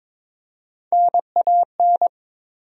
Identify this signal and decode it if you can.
Morse code, heard as a keyed tone, about 22 wpm, 700 Hz. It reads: NAN
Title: USA Common Names - Max 3 Letters - 22wpm